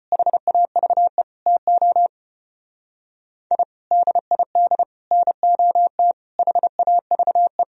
Morse code, heard as a keyed tone, about 33 wpm, 700 Hz. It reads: HAVE TO I DID NOT HAVE